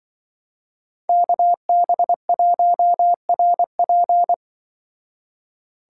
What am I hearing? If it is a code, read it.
Morse code, heard as a keyed tone, about 25 wpm, 700 Hz. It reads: KB1RP
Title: Speed Racing - USA 2x2 Call Signs 20wpm